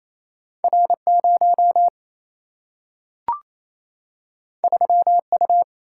Morse code, heard as a keyed tone, about 28 wpm, 700 Hz. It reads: R 0 3 U
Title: Single-Letter Single-Number 28wpm